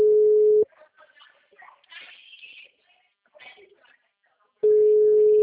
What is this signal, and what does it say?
Ringback tone (Europe, 425 Hz), with speech.